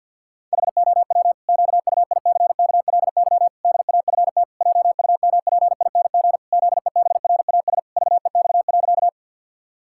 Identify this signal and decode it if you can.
Morse code, heard as a keyed tone, about 50 wpm, 700 Hz. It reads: HOW QUICKLY DAFT JUMPING ZEBRAS VEX.
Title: Pangram Challenge v2 - 50wpm